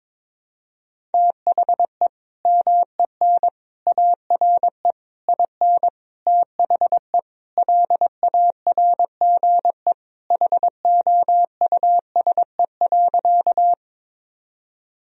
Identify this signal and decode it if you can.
Morse code, heard as a keyed tone, about 22 wpm, 700 Hz. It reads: THE MEN ARE IN THE LARGE HOUSE.
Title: Sentences from Top 1000 Words - Review Part 1 - 22wpm